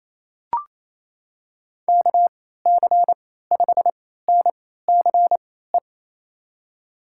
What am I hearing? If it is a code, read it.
Morse code, heard as a keyed tone, about 28 wpm, spaced as for 16 wpm, 700 Hz. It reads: KC5NCE
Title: USA Call Signs - 6 Characters - 3x Character Spacing - 28wpm